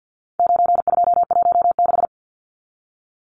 Morse code, heard as a keyed tone, about 50 wpm, 700 Hz. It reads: 9216
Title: Numbers - 4-Digit 50wpm